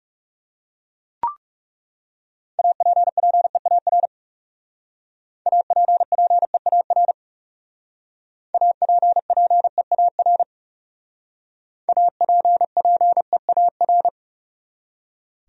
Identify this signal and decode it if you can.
Morse code, heard as a keyed tone, about 37 wpm, 700 Hz. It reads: APPEAR APPEAR APPEAR APPEAR
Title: Speed-Racing Top 500 Words 30wpm